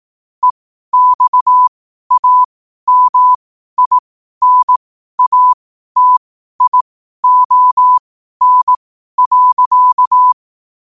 Morse code, heard as a keyed tone, about 18 wpm, spaced as for 12 wpm, 1000 Hz. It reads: EXAMINATION.
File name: 12wpm